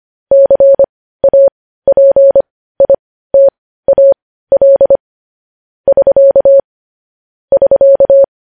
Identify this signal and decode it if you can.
Morse code, heard as a keyed tone, about 25 wpm, spaced as for 15 wpm, 550 Hz. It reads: CAPITAL <SK> <SK>
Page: News Headlines in Morse Code at 15 WPM